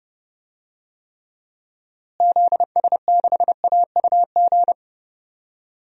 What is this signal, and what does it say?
Morse code, heard as a keyed tone, about 30 wpm, 700 Hz. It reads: ZS6AUG
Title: DXCC-100 Call Signs - 6 Characters - Repeated 2x - 30wpm